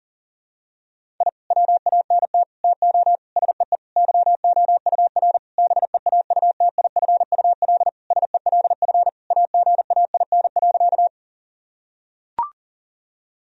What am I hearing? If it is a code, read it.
Morse code, heard as a keyed tone, about 40 wpm, 700 Hz. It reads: I WANT TO SEE YOUR BEAUTIFUL SELF AGAIN.
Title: Sentences from Top 1000 Words - Review Part 1 - 40wpm